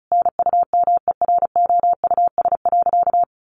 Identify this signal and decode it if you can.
Morse code, heard as a keyed tone, about 34 wpm, 700 Hz. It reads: NUMEROUS.
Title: Sentences from Top 1400 Words 35wpm